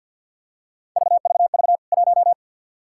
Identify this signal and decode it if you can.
Morse code, heard as a keyed tone, about 52 wpm, 700 Hz. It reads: VVV 1
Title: K1USN SST QSO Practice 50wpm